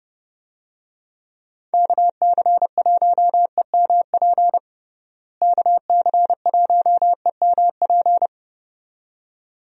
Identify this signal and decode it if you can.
Morse code, heard as a keyed tone, about 30 wpm, 700 Hz. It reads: KC1EMP KC1EMP
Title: USA Call Signs - 6 Characters - Repeated 2x - 30wpm